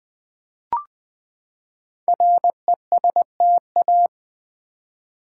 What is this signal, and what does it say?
Morse code, heard as a keyed tone, about 20 wpm, 700 Hz. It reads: RESTA
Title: Build A Word - 5 Letters At A Time - 10-Letter Words - 20wpm